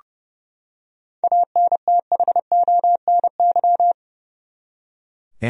Morse code, heard as a keyed tone, about 30 wpm, 700 Hz. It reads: ANTHONY
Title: USA Common Names - 30wpm